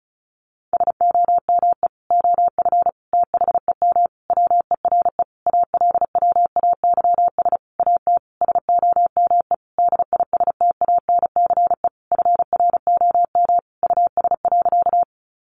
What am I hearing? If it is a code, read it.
Morse code, heard as a keyed tone, about 34 wpm, 700 Hz. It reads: SOME OF THEM WERE ALWAYS AT SOME DISTANCE FROM US.
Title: Sentences from Top 1200 Words 35wpm